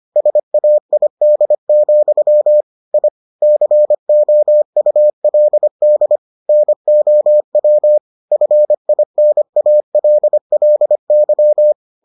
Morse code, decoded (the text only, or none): SAID, I COULD NOW FINALLY